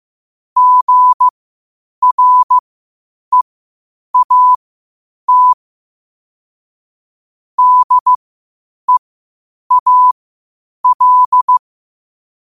Morse code, heard as a keyed tone, about 15 wpm, spaced as for 8 wpm, 1000 Hz. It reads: GREAT DEAL
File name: qotdfiles / 8wpm